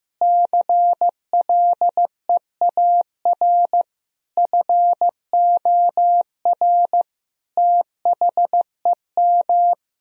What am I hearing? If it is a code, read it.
Morse code, heard as a keyed tone, about 15 wpm, 700 Hz. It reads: CLEAR FOR THEM
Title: Sentences from Top 1000 Words - Review Part 2 - 15wpm